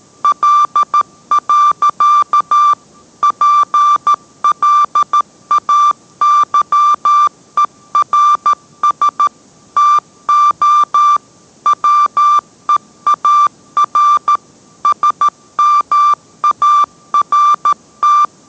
Morse code, heard as a keyed tone, about 13 wpm, 1180 Hz. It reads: L. PLAYERS TO WEAR SMART